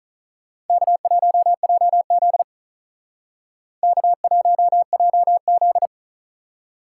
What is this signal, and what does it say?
Morse code, heard as a keyed tone, about 45 wpm, 700 Hz. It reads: K1JZ K1JZ